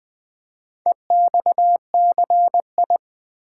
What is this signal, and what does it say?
Morse code, heard as a keyed tone, about 20 wpm, 700 Hz. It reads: EXCI
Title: Build A Word - 2 Letters At A Time - 8-Letter Words - 20wpm